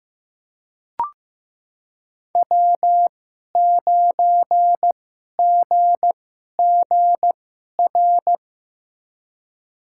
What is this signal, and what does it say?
Morse code, heard as a keyed tone, about 16 wpm, spaced as for 11 wpm, 700 Hz. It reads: W9GGR
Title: USA Call Signs - 5 Characters - 2x Character Spacing - 15wpm